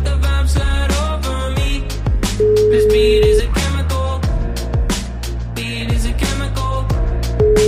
Ringback tone (Europe, 425 Hz), with noise.